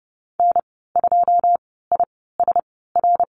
Morse code, heard as a keyed tone, about 30 wpm, spaced as for 17 wpm, 700 Hz. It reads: N2ISR
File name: Lesson 14g - Number 2 - Call Signs 30wpm